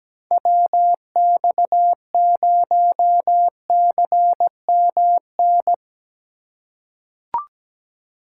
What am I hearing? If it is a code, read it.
Morse code, heard as a keyed tone, about 17 wpm, 700 Hz. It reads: WX0CMN
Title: USA Call Signs - 6 Characters - 3x Character Spacing - 17wpm